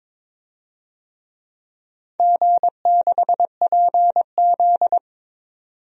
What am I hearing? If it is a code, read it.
Morse code, heard as a keyed tone, about 22 wpm, 700 Hz. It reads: G6PZ
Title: DXCC-100 Call Signs - 4 Characters - 22wpm